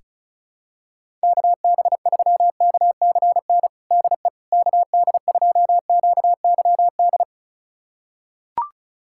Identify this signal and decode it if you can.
Morse code, heard as a keyed tone, about 34 wpm, 690 Hz. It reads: KB3KCN DE KD2QYD
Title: USA Callsign DE Callsign - 6 Characters - 35wpm